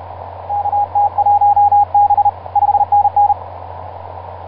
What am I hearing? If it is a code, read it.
Morse code, heard as a keyed tone, about 32 wpm, 820 Hz, with radio noise. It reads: KT1B 5NN
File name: pj2-oh1vr iz2amw etc 12m cw 24mar2010 1955